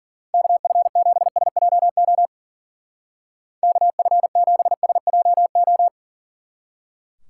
Morse code, heard as a keyed tone, about 49 wpm, 700 Hz. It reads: KF7SJO KF7SJO
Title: Speed Racing - USA 2x3 Call Signs 40wpm